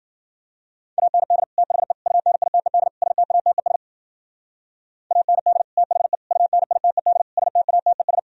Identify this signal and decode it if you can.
Morse code, heard as a keyed tone, about 49 wpm, 700 Hz. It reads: AND THE UNITED STATES AND THE UNITED STATES
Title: Speed-Racing - Sets of 4 Words 40wpm